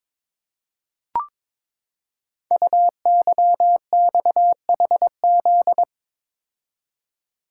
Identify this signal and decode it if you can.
Morse code, heard as a keyed tone, about 22 wpm, 700 Hz. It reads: UYXHZ
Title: Code Groups - 5 Letters - 22wpm